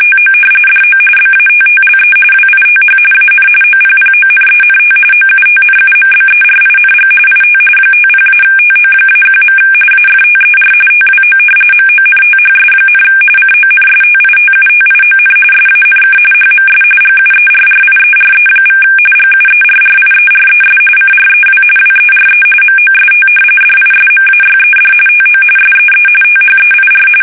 RTTY, about 45.45 baud. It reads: TO DO BATTLE WITH AN OPPONENT OUTNUMBERING IT 50-TO-1. AFTER TWO MONTHS, CONSTITUTION SANK HMS GUERRIERE= GUERRIERES CREW WERE MOST DISMAYED TO SEE THEIR
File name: fsk850 / usnavy2